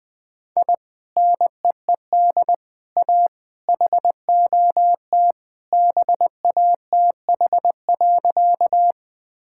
Morse code, heard as a keyed tone, about 21 wpm, 700 Hz. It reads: I NEED A HOT BATH.
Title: Sentences from Top 1600 Words 20wpm